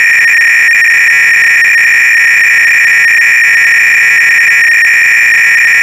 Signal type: 300-baud modem data